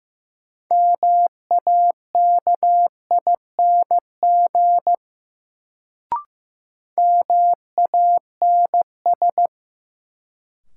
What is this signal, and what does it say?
Morse code, heard as a keyed tone, about 16 wpm, 700 Hz. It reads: MAKING MANS
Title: Sentences from Top 1000 Words - Review Part 6 - New Word Drill - 15wpm